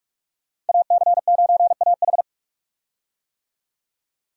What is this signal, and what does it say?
Morse code, heard as a keyed tone, about 46 wpm, 700 Hz. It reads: AC9AH
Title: USA Call Signs - 5 Characters - 45wpm